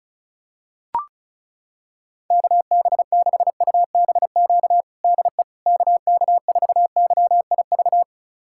Morse code, heard as a keyed tone, about 36 wpm, 690 Hz. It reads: KB6UBQ DE KK4YIV
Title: USA Callsign DE Callsign - 6 Characters - 35wpm